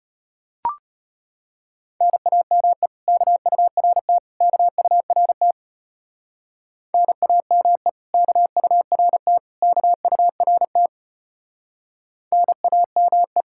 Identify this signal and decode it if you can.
Morse code, heard as a keyed tone, about 31 wpm, 700 Hz. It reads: NAME KURT KURT NAME KURT KURT NAME
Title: Speed-Racing Common QSO Phrases - 25wpm